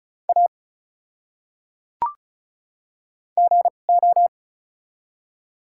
Morse code, heard as a keyed tone, about 34 wpm, spaced as for 25 wpm, 700 Hz. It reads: A GO
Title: 2x Character Spacing - Top 100 Words - 35wpm